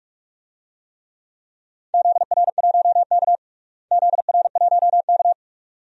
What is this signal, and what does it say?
Morse code, heard as a keyed tone, about 41 wpm, 700 Hz. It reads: ZR1K ZR1K
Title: DXCC-100 Call Signs - 4 Characters - Repeated 2x - 45wpm